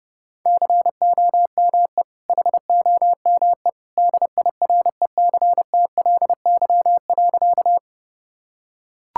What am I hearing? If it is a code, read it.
Morse code, heard as a keyed tone, about 30 wpm, 700 Hz. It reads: COME HOME DIRECTLY.
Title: Sentences from Top 1000 Words - Review Part 5 - 30wpm